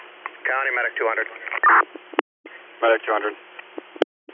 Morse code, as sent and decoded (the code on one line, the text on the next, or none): none